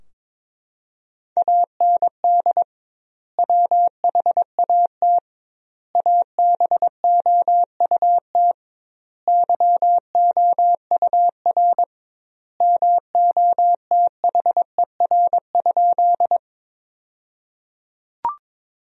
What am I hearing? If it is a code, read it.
Morse code, heard as a keyed tone, about 21 wpm, 700 Hz. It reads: AND WHAT ABOUT YOUR MOTHER?
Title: 2x Word-Spacing - Sentences from Top 200 Words - 22wpm